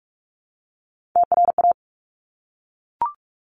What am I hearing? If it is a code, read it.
Morse code, heard as a keyed tone, about 46 wpm, 690 Hz. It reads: TRA
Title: Top Three Letter Combinations in Words 45wpm